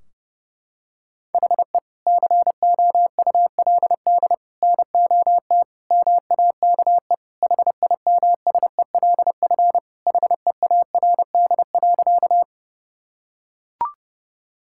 Morse code, heard as a keyed tone, about 30 wpm, 700 Hz. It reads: HE COULD NOT MAKE HIMSELF HEARD.
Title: Sentences from Top 1600 Words 30wpm